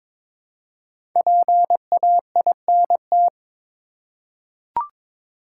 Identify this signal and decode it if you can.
Morse code, heard as a keyed tone, about 22 wpm, 700 Hz. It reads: PAINT
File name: 5 Letter Words - First 3 Letters Given - 22wpm